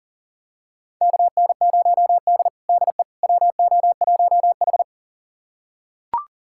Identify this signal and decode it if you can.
Morse code, heard as a keyed tone, about 37 wpm, 700 Hz. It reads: KN0D DE WO1H
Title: USA Callsign DE Callsign - 4 Characters - 40wpm